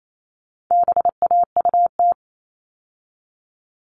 Morse code, heard as a keyed tone, about 28 wpm, 700 Hz. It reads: BAUT
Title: All Cut Numbers - 4-Digit 28wpm